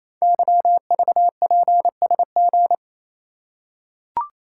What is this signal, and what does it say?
Morse code, heard as a keyed tone, about 27 wpm, 700 Hz. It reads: YVPSG
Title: Code Groups - 5 Letters - 28wpm